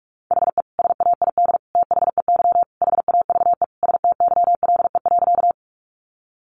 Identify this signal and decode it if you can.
Morse code, heard as a keyed tone, about 44 wpm, 700 Hz. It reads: HE SAID THEY HAVE STYLE.
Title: Sentences from Top 800 Words 45wpm